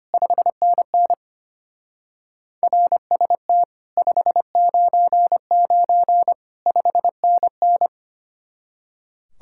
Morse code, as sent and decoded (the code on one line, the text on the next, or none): ..... -. -. / .-. ... - / ..... ----. ----. / ..... -. -.
5NN RST 599 5NN